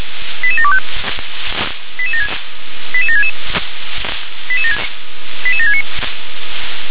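Five-tone selective call (ZVEI), heard as quick runs of five tones, tones of about 70 ms. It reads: 88725 88766 88769
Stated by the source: ZVEI five-tone call 88560 received on SDR.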